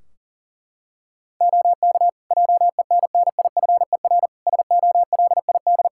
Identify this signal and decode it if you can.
Morse code, heard as a keyed tone, about 40 wpm, 700 Hz. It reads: OK JENNIFER SOLID
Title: Practice QSO - Exchange 3 - More Info - 40wpm